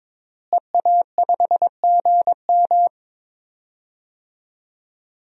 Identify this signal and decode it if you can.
Morse code, heard as a keyed tone, about 22 wpm, 700 Hz. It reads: EA5GM
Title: DXCC-100 Call Signs - 5 Characters - Repeated 2x - 22wpm